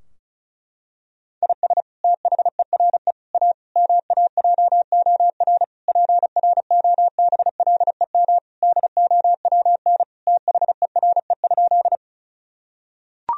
Morse code, heard as a keyed tone, about 36 wpm, 700 Hz. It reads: IS THERE A MAJOR PROBLEM DOWN THERE?
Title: Sentences from Top 1000 Words - Review Part 4 - 35wpm